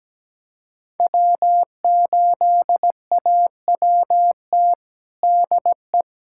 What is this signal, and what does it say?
Morse code, heard as a keyed tone, about 17 wpm, 700 Hz. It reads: W8AWT DE
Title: USA Callsign DE Callsign - 5 Characters - 17wpm